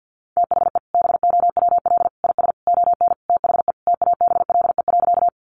Morse code, heard as a keyed tone, about 50 wpm, 700 Hz. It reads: THE BOWL IS ON THE TABLE.
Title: Sentences from Top 1200 Words 50wpm